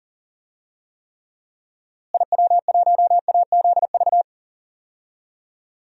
Morse code, heard as a keyed tone, about 40 wpm, 700 Hz. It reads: IW1AZV
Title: DXCC-100 Call Signs - 6 Characters - Repeated 3x - 40wpm